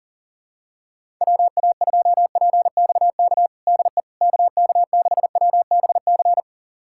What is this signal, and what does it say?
Morse code, heard as a keyed tone, about 40 wpm, 700 Hz. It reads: WA2PXK DE KK6WBC
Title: USA Callsign DE Callsign - 6 Characters - 40wpm